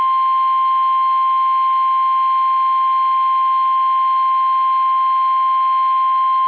Emergency Alert System (SAME) tones (the 1050 Hz attention tone). It is heard from 0.9 to 5.6 s.
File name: eas / KUSO / KUSO